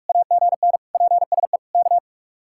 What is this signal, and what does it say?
Morse code, heard as a keyed tone, about 43 wpm, 700 Hz. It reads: AGN PSE K